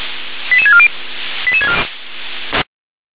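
Five-tone selective call (ZVEI), heard as quick runs of five tones, tones of about 70 ms. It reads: 88640
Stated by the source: ZVEI five-tone call received by SDR.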